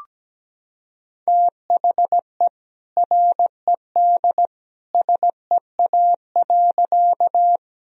Morse code, heard as a keyed tone, about 17 wpm, 700 Hz. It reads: THE RED SEA.
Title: Sentences from Top 1000 Words - Review Part 2 - 17wpm